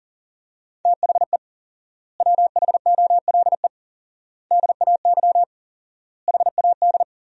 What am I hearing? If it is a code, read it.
Morse code, heard as a keyed tone, about 37 wpm, 700 Hz. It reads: THE WHOLE DAY HAD
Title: 4x Word-Spacing - Sentences from Top 500 Words - 40wpm